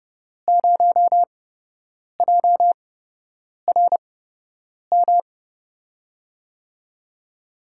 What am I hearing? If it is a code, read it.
Morse code, heard as a keyed tone, about 30 wpm, spaced as for 8 wpm, 700 Hz. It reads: 0JRM
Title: USA Call Signs - 5 Characters - 8x Character Spacing - 30wpm